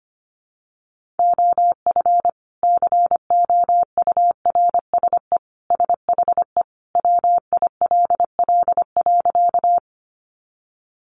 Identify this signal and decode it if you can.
Morse code, heard as a keyed tone, about 25 wpm, 700 Hz. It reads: OF COURSE SHE WILL.
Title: Sentences from Top 500 Words 25wpm